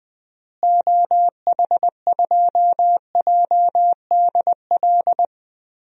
Morse code, heard as a keyed tone, about 20 wpm, 700 Hz. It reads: OH2JDL
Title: DXCC-100 Call Signs - 6 Characters - Repeated 2x - 20wpm